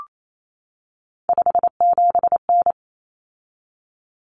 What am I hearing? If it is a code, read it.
Morse code, heard as a keyed tone, about 28 wpm, 700 Hz. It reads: 57N
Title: Common Cut Numbers - 3-Digit 28wpm